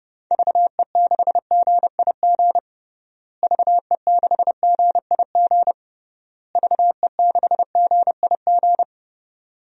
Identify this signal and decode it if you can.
Morse code, heard as a keyed tone, about 32 wpm, 700 Hz. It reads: VE6GIG VE6GIG VE6GIG
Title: DXCC-100 Call Signs - 6 Characters - Repeated 3x - 30wpm